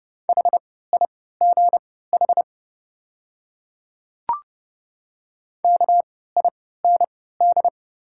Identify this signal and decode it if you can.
Morse code, heard as a keyed tone, about 29 wpm, spaced as for 17 wpm, 690 Hz. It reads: HIGH KIND
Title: 3x Character Spacing - Top 200 Words - 30wpm